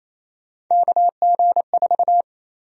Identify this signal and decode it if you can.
Morse code, heard as a keyed tone, about 28 wpm, 700 Hz. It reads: KG4
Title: USA Call Sign Prefixes - 28wpm